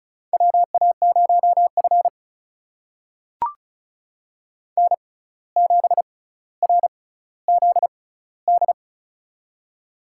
Morse code, heard as a keyed tone, about 36 wpm, spaced as for 17 wpm, 700 Hz. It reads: WA0F N7RZD
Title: USA Call Signs - 6x Character Spacing - 35wpm